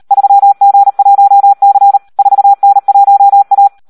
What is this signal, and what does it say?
Morse code, heard as a keyed tone, about 38 wpm, 800 Hz. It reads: 3G1C 4N1A
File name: Audio practica CW / SD 1 38